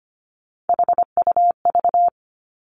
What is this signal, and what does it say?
Morse code, heard as a keyed tone, about 25 wpm, 700 Hz. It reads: HUV
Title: Three Challenging Letter-Number Combinations 25wpm